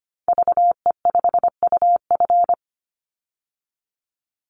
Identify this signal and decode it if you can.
Morse code, heard as a keyed tone, about 25 wpm, 700 Hz. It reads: VE5UF
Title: Call Signs Repeated 2x - 25wpm